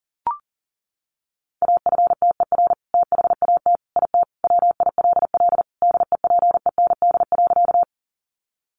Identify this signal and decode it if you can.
Morse code, heard as a keyed tone, about 40 wpm, 700 Hz. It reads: AFTER THAT IT WILL DEPEND.
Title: Sentences from Top 1300 Words 40wpm